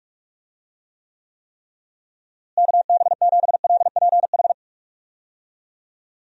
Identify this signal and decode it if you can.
Morse code, heard as a keyed tone, about 46 wpm, 700 Hz. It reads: KB7LPH
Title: USA Call Signs - 6 Characters - Repeated 3x - 45wpm